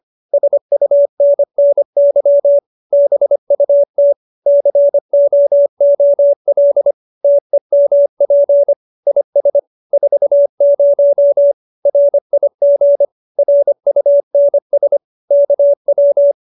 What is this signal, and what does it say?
Morse code, heard as a keyed tone, about 25 wpm, 570 Hz. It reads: SUNNY BUT COOL TEMP IS 40 RIG RUNS KW
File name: QSO302 25